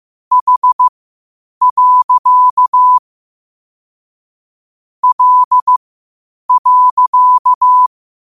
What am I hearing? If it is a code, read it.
Morse code, heard as a keyed tone, about 15 wpm, spaced as for 8 wpm, 1000 Hz. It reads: H. L.
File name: qotdfiles / 8wpm